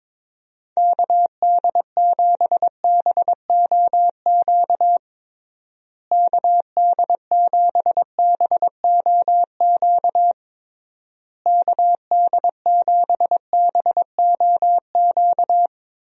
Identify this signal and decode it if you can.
Morse code, heard as a keyed tone, about 22 wpm, 700 Hz. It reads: KD7BOQ KD7BOQ KD7BOQ
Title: USA Call Signs - 6 Characters - Repeated 3x - 22wpm